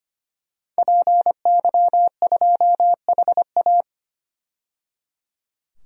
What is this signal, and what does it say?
Morse code, heard as a keyed tone, about 25 wpm, 700 Hz. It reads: PY2HA
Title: DXCC-100 Call Signs - 5 Characters - 25wpm